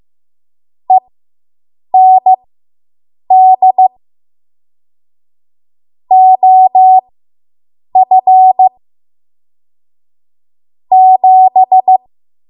Morse code, heard as a keyed tone, about 15 wpm, spaced as for 7 wpm, 750 Hz. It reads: END OF 7